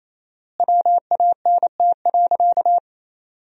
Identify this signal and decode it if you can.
Morse code, heard as a keyed tone, about 28 wpm, 700 Hz. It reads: WANT.
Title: 4x Word-Spacing - Sentences from Top 200 Words - 28wpm